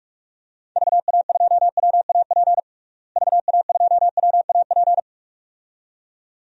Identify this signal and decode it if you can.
Morse code, heard as a keyed tone, about 41 wpm, 700 Hz. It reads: VA2WAP VA2WAP
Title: DXCC-100 Call Signs - 6 Characters - Repeated 2x - 45wpm